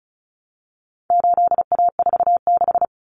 Morse code, heard as a keyed tone, about 34 wpm, 700 Hz. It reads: 8A46
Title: Common Cut Numbers - 4-Digit 35wpm